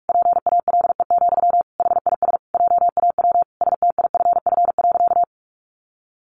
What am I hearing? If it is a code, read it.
Morse code, heard as a keyed tone, about 45 wpm, 700 Hz. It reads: PALE, HIS JAW STIFF.
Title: A Sound Of Thunder 45wpm